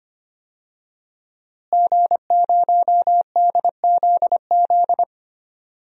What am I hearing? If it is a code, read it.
Morse code, heard as a keyed tone, about 25 wpm, 700 Hz. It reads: G0DZZ
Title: DXCC-100 Call Signs - 5 Characters - Repeated 3x - 25wpm